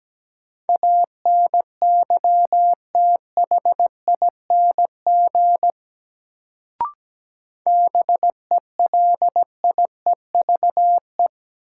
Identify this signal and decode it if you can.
Morse code, heard as a keyed tone, about 17 wpm, 700 Hz. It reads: ANYTHING BELIEVE
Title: Sentences from Top 1000 Words - New Word Drill - 17wpm